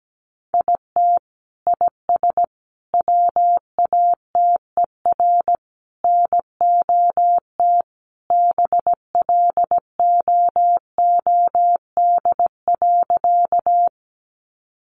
Morse code, heard as a keyed tone, about 17 wpm, 700 Hz. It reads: IT IS WATER NOT BLOOD.
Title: Sentences from Top 1100 Words 17wpm